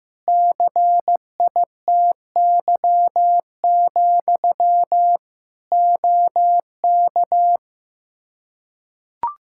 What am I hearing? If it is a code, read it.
Morse code, heard as a keyed tone, about 15 wpm, 700 Hz. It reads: CITY, OK
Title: Practice QSO - Exchange 1 - Beginning - 15wpm